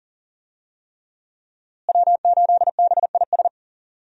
Morse code, heard as a keyed tone, about 40 wpm, 700 Hz. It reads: W8BIS
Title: USA Call Signs - 5 Characters - Repeated 3x - 40wpm